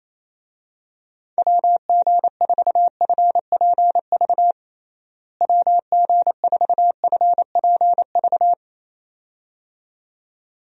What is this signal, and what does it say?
Morse code, heard as a keyed tone, about 28 wpm, 700 Hz. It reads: WG4FPV WG4FPV
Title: USA Call Signs - 6 Characters - Repeated 2x - 28wpm